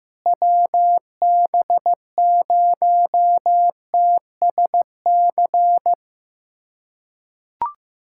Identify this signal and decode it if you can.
Morse code, heard as a keyed tone, about 15 wpm, 700 Hz. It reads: WB0TSC
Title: USA Callsign DE Callsign 15wpm